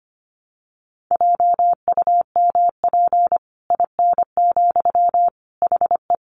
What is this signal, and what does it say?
Morse code, heard as a keyed tone, about 25 wpm, 700 Hz. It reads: JUMP IN, HE
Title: Sentences from Top 1300 Words 25wpm